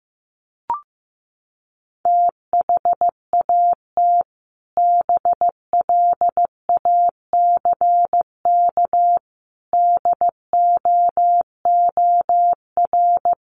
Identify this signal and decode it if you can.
Morse code, heard as a keyed tone, about 15 wpm, 700 Hz. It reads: THAT BLACK DOOR
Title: Sentences from Top 1400 Words 15wpm